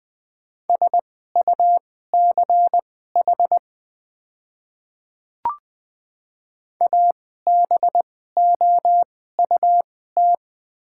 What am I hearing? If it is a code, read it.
Morse code, heard as a keyed tone, about 21 wpm, spaced as for 15 wpm, 690 Hz. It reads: SUCH ABOUT
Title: 2x Character Spacing - Top 200 Words - 20wpm